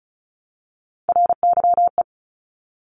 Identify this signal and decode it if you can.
Morse code, heard as a keyed tone, about 36 wpm, 700 Hz. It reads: RYE
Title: Multi-letter Phonemes Repeated 3x 35wpm